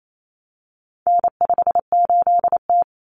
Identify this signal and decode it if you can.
Morse code, heard as a keyed tone, about 28 wpm, 700 Hz. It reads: N58T
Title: Common Cut Numbers - 4-Digit 28wpm